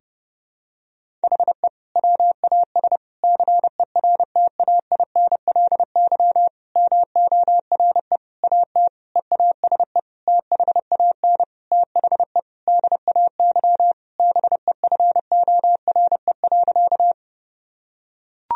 Morse code, heard as a keyed tone, about 30 wpm, 700 Hz. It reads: HE WAS CERTAINLY MORE AT EASE THAN THE DAY BEFORE.
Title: Sentences from Top 1000 Words - Review Part 1 - 30wpm